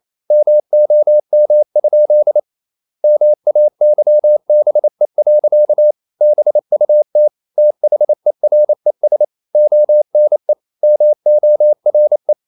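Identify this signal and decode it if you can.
Morse code, heard as a keyed tone, about 28 wpm, 600 Hz. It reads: MOM? MAYBE. BUT THERES ONE MORE
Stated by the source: Morse code practice at 28 wpm, text of Ken Follett's novel Never in English, from CW med Gnister.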